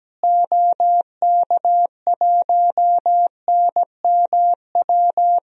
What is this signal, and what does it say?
Morse code, heard as a keyed tone, about 17 wpm, 700 Hz. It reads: OK1NMW
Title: DXCC-100 Call Signs - 6 Characters - Repeated 3x - 17wpm